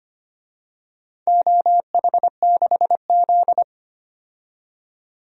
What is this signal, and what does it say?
Morse code, heard as a keyed tone, about 25 wpm, 700 Hz. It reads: OH6Z
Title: DXCC-100 Call Signs - 4 Characters - 25wpm